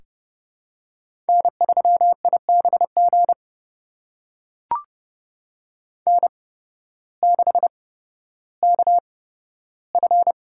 Morse code, heard as a keyed tone, about 30 wpm, spaced as for 14 wpm, 700 Hz. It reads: N3IBG N6KF
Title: USA Call Signs - 5 Characters - 8x Character Spacing - 30wpm